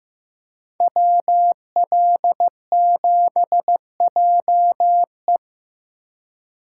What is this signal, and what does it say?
Morse code, heard as a keyed tone, about 19 wpm, 700 Hz. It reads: WL7JE
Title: Speed Racing - USA 2x2 Call Signs 15wpm